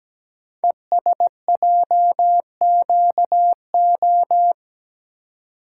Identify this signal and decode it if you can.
Morse code, heard as a keyed tone, about 17 wpm, 700 Hz. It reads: ESJQO
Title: Code Groups - 5 Letters - 17wpm